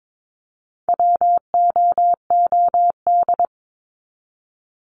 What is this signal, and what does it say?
Morse code, heard as a keyed tone, about 22 wpm, 700 Hz. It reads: WOOD
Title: Top 300 Words 22wpm